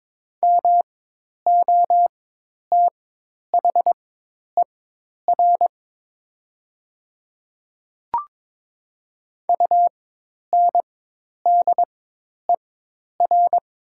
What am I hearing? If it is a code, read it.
Morse code, heard as a keyed tone, about 22 wpm, spaced as for 10 wpm, 700 Hz. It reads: MOTHER UNDER
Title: 4x Character Spacing - Top 300 Words - 22wpm